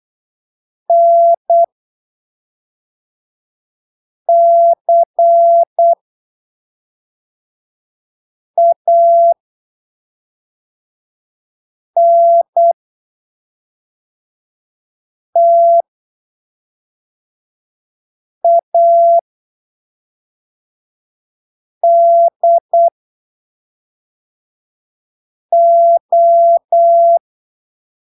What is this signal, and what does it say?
Morse code, heard as a keyed tone, about 8 wpm, spaced as for 3 wpm, 680 Hz. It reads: NCANTADO
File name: arquivos / texto / texto 29 08ppm